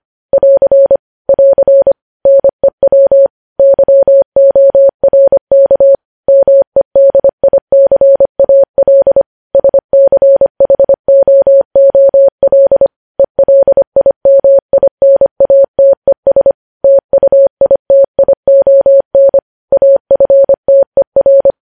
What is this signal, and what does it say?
Morse code, heard as a keyed tone, about 25 wpm, 550 Hz. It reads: <AR> <AR> NEW YORK MEDICAL SCHOOL ELIMINATES TUITION AFTER
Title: News Headlines at 25